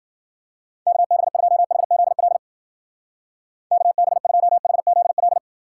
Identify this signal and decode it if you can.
Morse code, heard as a keyed tone, about 52 wpm, 700 Hz. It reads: KB2HZL KB2HZL